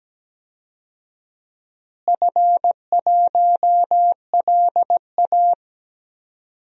Morse code, heard as a keyed tone, about 17 wpm, 700 Hz. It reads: F1LA
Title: DXCC-100 Call Signs - 4 Characters - Repeated 3x - 17wpm